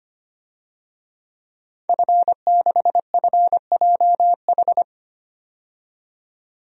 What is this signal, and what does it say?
Morse code, heard as a keyed tone, about 25 wpm, 700 Hz. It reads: F6FJH
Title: DXCC-100 Call Signs - 5 Characters - 25wpm